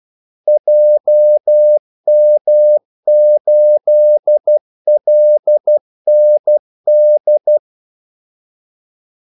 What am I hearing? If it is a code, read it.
Morse code, heard as a keyed tone, about 12 wpm, 600 Hz. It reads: JM8LND
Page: 2 ekstra ordmellemrum mellem kaldesignalerne